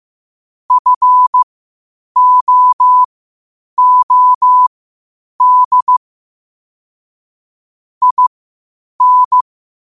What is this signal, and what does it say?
Morse code, heard as a keyed tone, about 15 wpm, spaced as for 8 wpm, 1000 Hz. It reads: FOOD IN
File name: qotdfiles / 8wpm